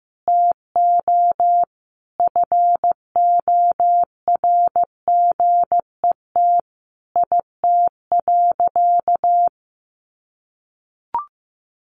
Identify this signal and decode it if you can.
Morse code, heard as a keyed tone, about 15 wpm, 700 Hz. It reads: TO FORGET IT.
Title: Sentences from Top 1200 Words 15wpm